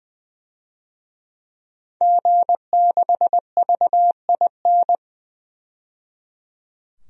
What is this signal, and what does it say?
Morse code, heard as a keyed tone, about 20 wpm, 700 Hz. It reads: G6VIN
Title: DXCC-100 Call Signs - 5 Characters - 20wpm